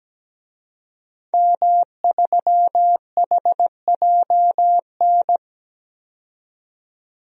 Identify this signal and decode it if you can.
Morse code, heard as a keyed tone, about 17 wpm, 700 Hz. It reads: M3HJN
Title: DXCC-100 Call Signs - 5 Characters - 17wpm